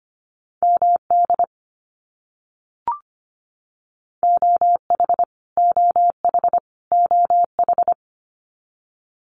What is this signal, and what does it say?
Morse code, heard as a keyed tone, about 25 wpm, 700 Hz. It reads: MD OH OH OH
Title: US States Repeated 3x 25wpm